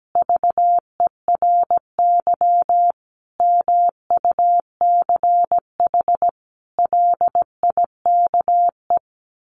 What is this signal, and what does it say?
Morse code, heard as a keyed tone, about 17 wpm, 700 Hz. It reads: VERY MUCH LIKE
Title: Sentences from Top 1300 Words 17wpm